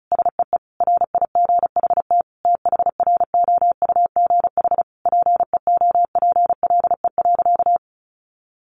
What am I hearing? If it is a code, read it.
Morse code, heard as a keyed tone, about 34 wpm, 700 Hz. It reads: SEE RIGHT THROUGH PEOPLE.
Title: Sentences from Top 200 Words 35wpm